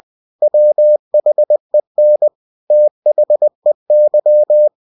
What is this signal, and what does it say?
Morse code, heard as a keyed tone, about 20 wpm, 600 Hz. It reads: WHEN THEY
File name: cw-books / winnie-the-pooh / winnie-20wpm-page-0034